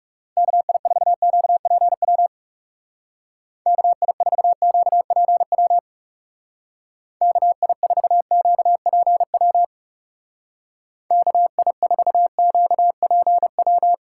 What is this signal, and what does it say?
Morse code, heard as a keyed tone, about 36 wpm, 700 Hz. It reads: KI4QPW KI4QPW KI4QPW KI4QPW
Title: Speed Racing - USA Call Signs 30wpm